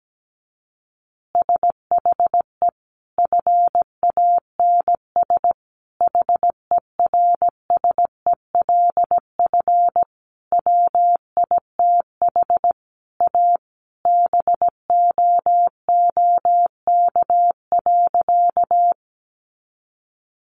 Morse code, heard as a keyed tone, about 17 wpm, 700 Hz. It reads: SHE FANS HERSELF WITH A BOOK.
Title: Additional Easy Phrases 17wpm